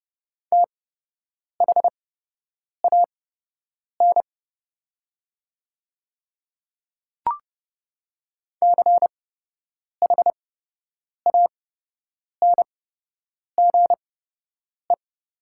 Morse code, heard as a keyed tone, about 29 wpm, spaced as for 8 wpm, 690 Hz. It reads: THAN CHANGE
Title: 8x Character Spacing - Top 300 Words - 30wpm